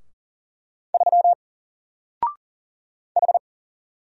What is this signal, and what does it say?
Morse code, heard as a keyed tone, about 40 wpm, 700 Hz. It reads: 3 H
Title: Single Letter-Number Alt 40wpm